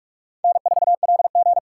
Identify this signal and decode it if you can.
Morse code, heard as a keyed tone, about 46 wpm, 700 Hz. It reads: N4LG
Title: USA Call Signs - 4 Characters - 4x Character Spacing - 45wpm